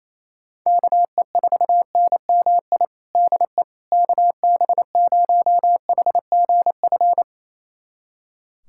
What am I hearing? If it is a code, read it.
Morse code, heard as a keyed tone, about 27 wpm, 700 Hz. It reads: KE4NMI DE KB0HGF
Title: USA Callsign DE Callsign - 6 Characters - 28wpm